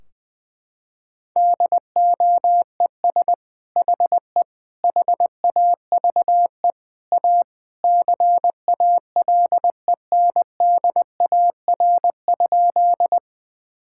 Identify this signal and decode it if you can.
Morse code, heard as a keyed tone, about 20 wpm, 700 Hz. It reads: DOES HE HAVE A CALENDAR?
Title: Sentences from Top 1600 Words 20wpm